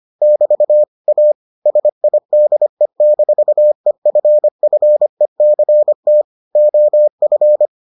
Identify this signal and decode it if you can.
Morse code, heard as a keyed tone, about 25 wpm, 600 Hz. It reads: <BT> A SIDE-EFFECT OF